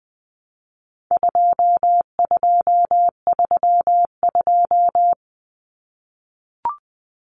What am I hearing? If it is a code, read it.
Morse code, heard as a keyed tone, about 20 wpm, 700 Hz. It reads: 2232
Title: Common Cut Numbers - 4-Digit 20wpm